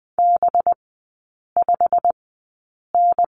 Morse code, heard as a keyed tone, about 20 wpm, spaced as for 8 wpm, 700 Hz. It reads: B5N
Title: 184 Lesson 30e - Letter b - Three Characters 20wpm